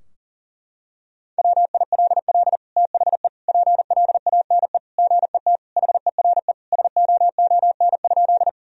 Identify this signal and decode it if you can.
Morse code, heard as a keyed tone, about 40 wpm, 700 Hz. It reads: WILL THE PLANE GET HERE SOON?
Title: Sentences from Top 1000 Words - Review Part 3 - 40wpm